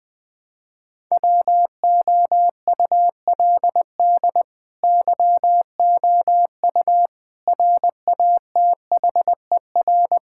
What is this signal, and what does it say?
Morse code, heard as a keyed tone, about 20 wpm, 700 Hz. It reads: WOULD YOU RATHER
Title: Sentences from Top 1000 Words - Review Part 6 - 20wpm